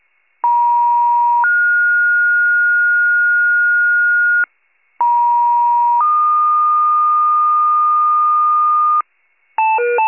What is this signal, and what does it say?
Two-tone page, heard as a one-second tone, then a three-second tone, about 950 then 1430 Hz, about 950 then 1220 Hz, with noise. It is heard from 0.4 to 9.0 s.
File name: 83 RESCUE